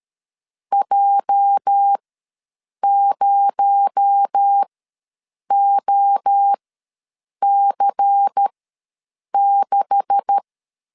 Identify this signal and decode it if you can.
Morse code, heard as a keyed tone, about 13 wpm, spaced as for 7 wpm, 800 Hz. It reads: J0OC6
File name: LEZIONI / 9 286 ws 12wpm 3351